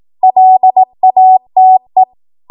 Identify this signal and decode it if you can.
Morse code, heard as a keyed tone, about 18 wpm, 750 Hz. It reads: LATE
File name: Morse / 18WPM